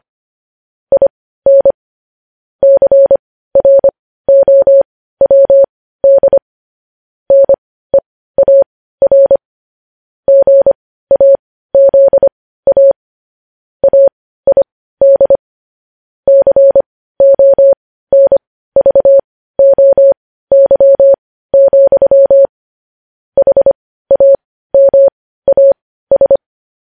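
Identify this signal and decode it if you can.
Morse code, heard as a keyed tone, about 25 wpm, spaced as for 15 wpm, 550 Hz. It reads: IN CROWD NEAR GAZA AID CONVOY, HAMAS
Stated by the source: Morse news headlines practice at 15 wpm.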